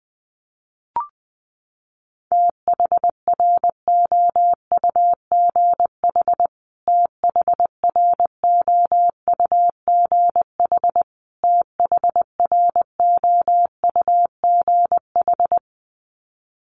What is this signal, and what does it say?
Morse code, heard as a keyed tone, about 19 wpm, 700 Hz. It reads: THROUGH THROUGH THROUGH
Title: Top 500 Words Repeated 20wpm